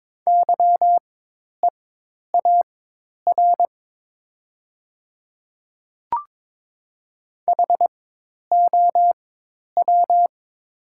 Morse code, heard as a keyed tone, about 22 wpm, spaced as for 10 wpm, 700 Hz. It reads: YEAR HOW
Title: 4x Character Spacing - Top 200 Words - 22wpm